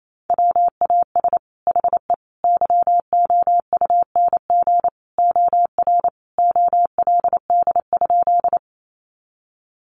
Morse code, heard as a keyed tone, about 28 wpm, 700 Hz. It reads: WAS HE YOUNG OR OLD?
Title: Sentences from Top 400 Words 28wpm